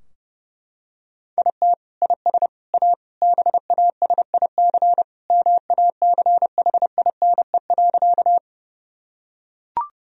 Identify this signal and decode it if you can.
Morse code, heard as a keyed tone, about 30 wpm, 700 Hz. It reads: IT IS A BASIC MACHINE.
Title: Sentences from Top 1000 Words - Review Part 3 - 30wpm